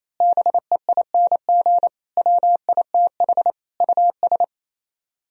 Morse code, heard as a keyed tone, about 26 wpm, 700 Hz. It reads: BEING WITH US
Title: Speed-Racing - Sets of 5 Words 28wpm